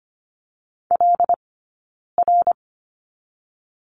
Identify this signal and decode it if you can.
Morse code, heard as a keyed tone, about 25 wpm, 700 Hz. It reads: L R
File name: Lesson 15d - Number 5 - Two Characters 25wpm